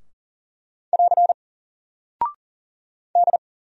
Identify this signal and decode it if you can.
Morse code, heard as a keyed tone, about 39 wpm, 700 Hz. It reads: <AR> D
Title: Single Character (Letter:Number:Punctuation:Prosign) 40wpm